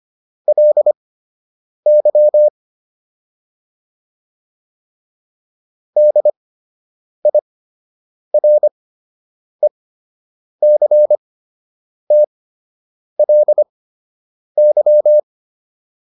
Morse code, heard as a keyed tone, about 25 wpm, spaced as for 8 wpm, 600 Hz. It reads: LY DIRECTLY